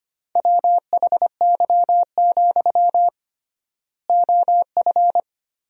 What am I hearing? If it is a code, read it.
Morse code, heard as a keyed tone, about 25 wpm, 700 Hz. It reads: WHY, OF
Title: 3x Word-Spacing - Sentences from Top 500 Words - 25wpm